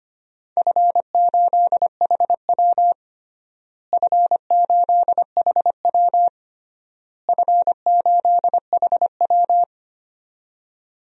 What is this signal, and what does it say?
Morse code, heard as a keyed tone, about 25 wpm, 700 Hz. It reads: F8HW F8HW F8HW
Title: DXCC-100 Call Signs - 4 Characters - Repeated 3x - 25wpm